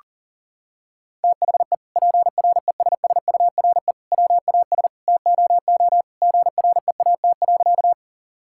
Morse code, heard as a keyed tone, about 40 wpm, 700 Hz. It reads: THE PRESSURE WAS TOO GREAT.
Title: Sentences from Top 1000 Words - Review Part 6 - 40wpm